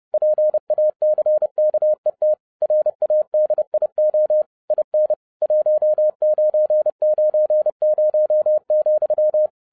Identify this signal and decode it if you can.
Morse code, heard as a keyed tone, about 30 wpm, 610 Hz. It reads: PACKET RADIO IN 1990,
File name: 30wpm